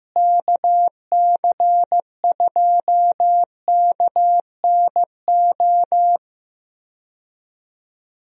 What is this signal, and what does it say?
Morse code, heard as a keyed tone, about 16 wpm, 700 Hz. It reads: KC2KNO
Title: USA Call Signs - 6 Characters - Repeated 2x - 15wpm